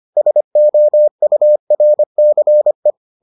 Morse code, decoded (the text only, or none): SOURCE